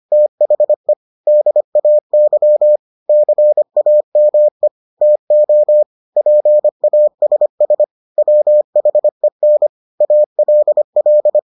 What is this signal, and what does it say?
Morse code, heard as a keyed tone, about 25 wpm, 600 Hz. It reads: THE DAY CAME TO PASS WHEN ALL